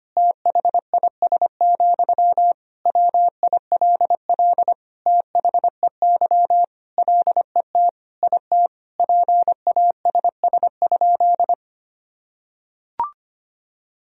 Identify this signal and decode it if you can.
Morse code, heard as a keyed tone, about 25 wpm, 700 Hz. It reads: THIS, WILL THEY LET IT PASS?
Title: Sentences from Top 400 Words and Abbreviations 25wpm